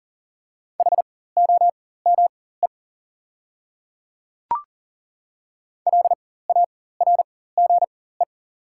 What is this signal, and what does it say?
Morse code, heard as a keyed tone, about 40 wpm, spaced as for 19 wpm, 700 Hz. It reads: HOME LARGE
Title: 4x Character Spacing - Top 500 Words - 40wpm